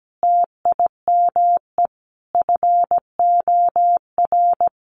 Morse code, heard as a keyed tone, about 17 wpm, 700 Hz. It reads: TIME FOR
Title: Sentences from Top 1100 Words 17wpm